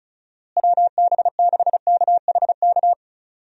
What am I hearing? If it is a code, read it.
Morse code, heard as a keyed tone, about 34 wpm, 700 Hz. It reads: WB6KHK
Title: USA Call Signs - 6 Characters - Repeated 3x - 35wpm